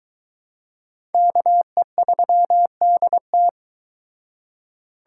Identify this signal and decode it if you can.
Morse code, heard as a keyed tone, about 23 wpm, 700 Hz. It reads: KE3DT